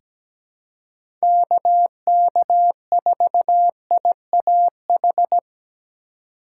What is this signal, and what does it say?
Morse code, heard as a keyed tone, about 17 wpm, 700 Hz. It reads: KK4IAH
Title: USA Call Signs - 6 Characters - 2x Character Spacing - 17wpm